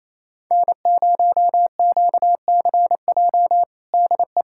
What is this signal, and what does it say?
Morse code, heard as a keyed tone, about 28 wpm, 700 Hz. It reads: N0QCJ DE
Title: Practice QSO - Exchange 4 - Ending - 28wpm